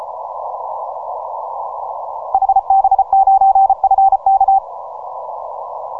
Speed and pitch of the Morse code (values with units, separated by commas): 33 wpm, 780 Hz